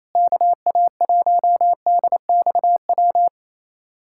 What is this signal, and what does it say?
Morse code, heard as a keyed tone, about 28 wpm, 700 Hz. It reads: KA1DXW
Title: USA Callsign DE Callsign - 6 Characters - 28wpm